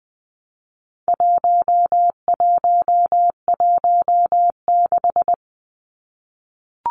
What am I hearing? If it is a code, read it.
Morse code, heard as a keyed tone, about 20 wpm, 700 Hz. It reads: 1116
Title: Numbers - 4-Digit 20wpm